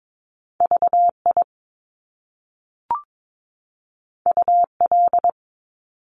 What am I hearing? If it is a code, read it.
Morse code, heard as a keyed tone, about 22 wpm, 700 Hz. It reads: VI UL
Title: Top Two Letter Combinations in Words 22wpm